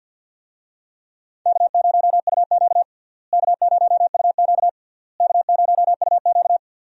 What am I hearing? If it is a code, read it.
Morse code, heard as a keyed tone, about 52 wpm, 700 Hz. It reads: K0UQ K0UQ K0UQ
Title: USA Call Signs - 4 Characters - Repeated 3x - 50wpm